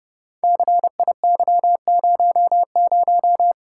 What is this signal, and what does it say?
Morse code, heard as a keyed tone, about 28 wpm, 700 Hz. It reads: CIY00
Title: Code Groups - 5 Letter:Numbers - 30wpm